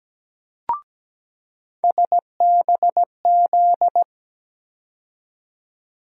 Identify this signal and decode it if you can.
Morse code, heard as a keyed tone, about 17 wpm, 700 Hz. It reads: SBZ
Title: USA Call Sign Postfixes - 3 Letters 17wpm